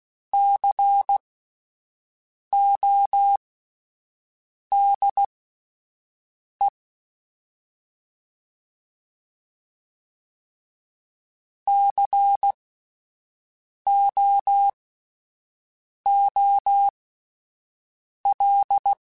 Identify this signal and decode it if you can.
Morse code, heard as a keyed tone, about 16 wpm, spaced as for 5 wpm, 800 Hz. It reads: CODE COOL
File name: Spaced Alphas / Spaced Alphas mp3s / Cs sp @10